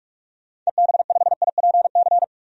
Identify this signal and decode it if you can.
Morse code, heard as a keyed tone, about 46 wpm, 700 Hz. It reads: EB5IPC
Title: DXCC-100 Call Signs - 6 Characters - 45wpm